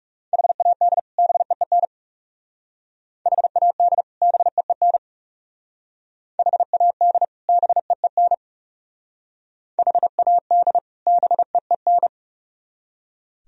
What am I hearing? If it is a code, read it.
Morse code, heard as a keyed tone, about 38 wpm, 700 Hz. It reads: HAD BEEN HAD BEEN HAD BEEN HAD BEEN